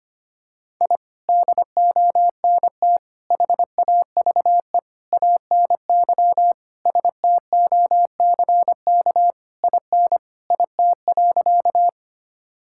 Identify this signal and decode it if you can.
Morse code, heard as a keyed tone, about 25 wpm, 700 Hz. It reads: I DONT HAVE ANY STOCK IN IT.
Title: Sentences from Top 1000 Words - Review Part 5 - 25wpm